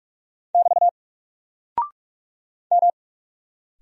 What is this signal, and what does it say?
Morse code, heard as a keyed tone, about 44 wpm, 700 Hz. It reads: <BT> M
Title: Single Character (Letter:Number:Punctuation:Prosign) Alt 45wpm